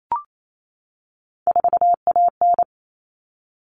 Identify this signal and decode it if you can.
Morse code, heard as a keyed tone, about 28 wpm, 700 Hz. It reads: 4AN
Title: All Cut Numbers - 3-Digit 28wpm